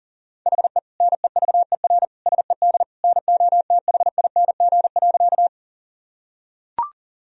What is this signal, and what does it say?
Morse code, heard as a keyed tone, about 40 wpm, 700 Hz. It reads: HE NEVER SED NOTHING.
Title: Sentences from Top 500 Words and Abbreviations 40wpm